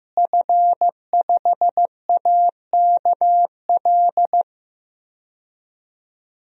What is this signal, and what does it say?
Morse code, heard as a keyed tone, about 15 wpm, 700 Hz. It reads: F5AKL
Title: DXCC-100 Call Signs - 5 Characters - Repeated 2x - 15wpm